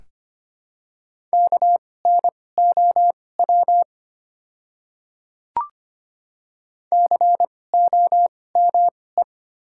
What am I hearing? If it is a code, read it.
Morse code, heard as a keyed tone, about 25 wpm, spaced as for 18 wpm, 700 Hz. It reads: KNOW COME
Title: 2x Character Spacing - Top 100 Words - 25wpm